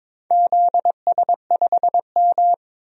Morse code, heard as a keyed tone, about 22 wpm, 700 Hz. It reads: ZS5M
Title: DXCC-100 Call Signs - 4 Characters - 22wpm